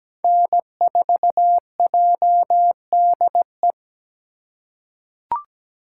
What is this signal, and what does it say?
Morse code, heard as a keyed tone, about 17 wpm, 700 Hz. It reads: N4JDE
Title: USA Call Signs - 5 Characters - 3x Character Spacing - 17wpm